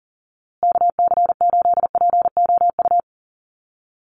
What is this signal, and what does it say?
Morse code, heard as a keyed tone, about 40 wpm, 700 Hz. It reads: KC8POU
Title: USA Call Signs Repeated 3x 40wpm